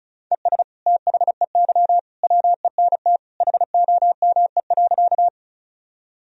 Morse code, heard as a keyed tone, about 36 wpm, 700 Hz. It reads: ES THEY WENT HOME.
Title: Sentences from Top 200 Words and Abbreviations 35wpm